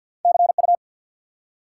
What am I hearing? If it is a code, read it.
Morse code, heard as a keyed tone, about 53 wpm, 700 Hz. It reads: CU
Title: Top 180 QSO Elements 50wpm